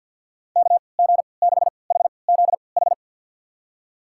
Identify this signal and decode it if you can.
Morse code, heard as a keyed tone, about 50 wpm, spaced as for 28 wpm, 700 Hz. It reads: KG6HZH
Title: USA Call Signs - 6 Characters - 3x Character Spacing - 50wpm